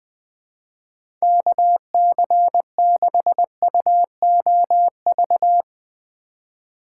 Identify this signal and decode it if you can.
Morse code, heard as a keyed tone, about 20 wpm, 700 Hz. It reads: KC6UOV
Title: USA Call Signs - 6 Characters - 3x Character Spacing - 20wpm